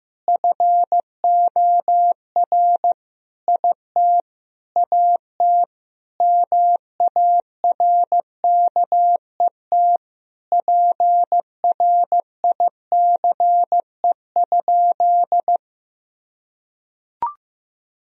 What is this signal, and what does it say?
Morse code, heard as a keyed tone, about 15 wpm, 700 Hz. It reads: FOR IT AT MARKET PRICE?
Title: Sentences from Top 1000 Words - Review Part 3 - 15wpm